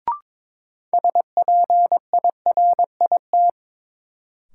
Morse code, heard as a keyed tone, about 22 wpm, 700 Hz. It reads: SPIRIT
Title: Sentences from Top 1400 Words - New Word Drill - 22wpm